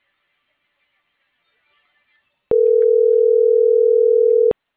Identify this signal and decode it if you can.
Ringback tone (North America), 440 + 480 Hz.